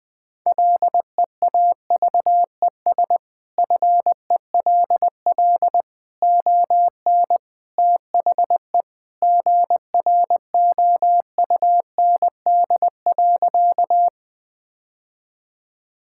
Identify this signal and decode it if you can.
Morse code, heard as a keyed tone, about 20 wpm, 700 Hz. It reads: LEAVES FELL ON THE GROUND.
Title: Sentences with 3-Word Phrases - Spoken Beginning - 20wpm